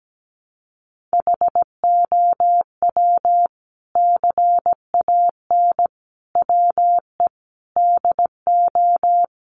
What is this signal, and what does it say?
Morse code, heard as a keyed tone, about 17 wpm, 700 Hz. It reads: HOW CAN WE DO
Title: Sentences from Top 600 Words 17wpm